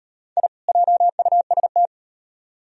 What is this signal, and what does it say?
Morse code, heard as a keyed tone, about 38 wpm, 700 Hz. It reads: I JUST